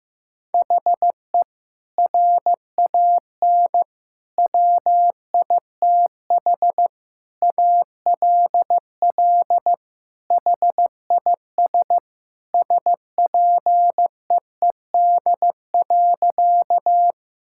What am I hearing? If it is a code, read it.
Morse code, heard as a keyed tone, about 15 wpm, 700 Hz. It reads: HE RAN WITH ALL HIS SPEED.
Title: Sentences from Top 1000 Words - Review Part 5 - 15wpm